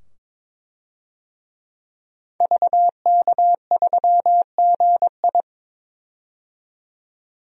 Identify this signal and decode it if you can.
Morse code, heard as a keyed tone, about 22 wpm, 700 Hz. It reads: VK3GI
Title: DXCC-100 Call Signs - 5 Characters - 22wpm